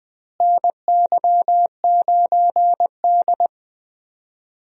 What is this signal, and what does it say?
Morse code, heard as a keyed tone, about 20 wpm, 700 Hz. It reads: NY9D
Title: USA Call Signs - 4 Characters - Repeated 2x - 20wpm